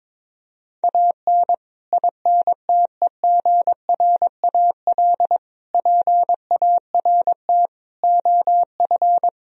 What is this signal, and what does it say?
Morse code, heard as a keyed tone, about 22 wpm, 700 Hz. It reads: AN INTEGRAL PART OF
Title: Speed-Racing - Sets of 5 Words 22wpm